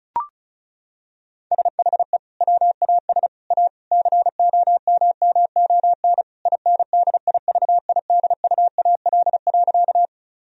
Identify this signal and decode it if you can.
Morse code, heard as a keyed tone, about 36 wpm, 700 Hz. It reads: SHE WAS A COMMON INDIVIDUAL.
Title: Sentences from Top 1000 Words - Review Part 4 - 35wpm